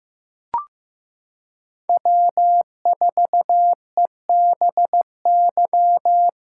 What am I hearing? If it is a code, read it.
Morse code, heard as a keyed tone, about 15 wpm, 700 Hz. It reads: W4EBY
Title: USA Callsign DE Callsign - 5 Characters - 15wpm